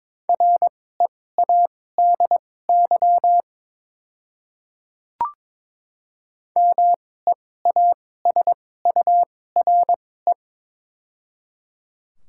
Morse code, heard as a keyed tone, about 22 wpm, spaced as for 16 wpm, 690 Hz. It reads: READY MEASURE
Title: 2x Character Spacing - Top 500 Words - 22wpm